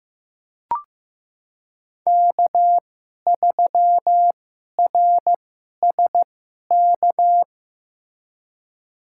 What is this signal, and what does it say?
Morse code, heard as a keyed tone, about 15 wpm, spaced as for 11 wpm, 700 Hz. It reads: K3RSK
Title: USA Call Signs - 5 Characters - 2x Character Spacing - 15wpm